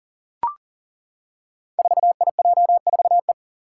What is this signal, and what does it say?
Morse code, heard as a keyed tone, about 41 wpm, 700 Hz. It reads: 4IJ4E
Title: Five Challenging Letter-Number Combinations 40wpm